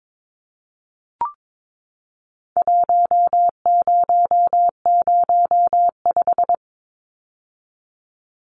Morse code, heard as a keyed tone, about 22 wpm, 700 Hz. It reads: 1005
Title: Numbers - Contest Style - 4-Digit 22wpm